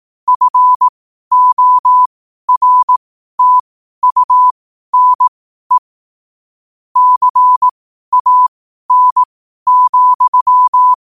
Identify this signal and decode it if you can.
Morse code, heard as a keyed tone, about 18 wpm, spaced as for 12 wpm, 1000 Hz. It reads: FORTUNE CAN,
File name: qotdfiles / 12wpm